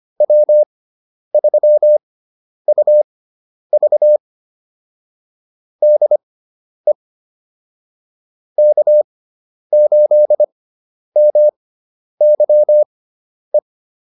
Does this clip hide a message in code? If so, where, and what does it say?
Yes: Morse code from 0.2 to 13.6 s, heard as a keyed tone, about 25 wpm, spaced as for 10 wpm, 600 Hz. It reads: W3UV DE K8MYE